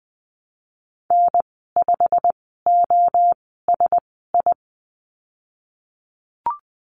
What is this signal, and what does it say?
Morse code, heard as a keyed tone, about 20 wpm, spaced as for 15 wpm, 700 Hz. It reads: N5OSI
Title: 088 Lesson 15g - Number 5 - Call Signs 20wpm